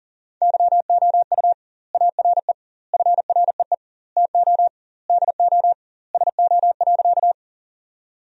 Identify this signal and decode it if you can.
Morse code, heard as a keyed tone, about 40 wpm, 700 Hz. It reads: YOU ARE FREE TO DO SO.
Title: 2x Word-Spacing - Sentences from Top 500 Words - 40wpm